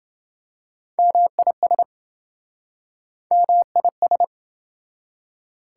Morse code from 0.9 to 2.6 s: MIS

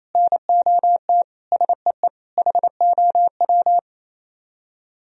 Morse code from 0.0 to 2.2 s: NOT SEE

